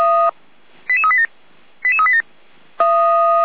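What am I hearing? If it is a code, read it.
Five-tone selective call (ZVEI), heard as quick runs of five tones, tones of about 70 ms. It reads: 88387 88387
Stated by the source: Five-tone ZVEI selective call received on an SDR.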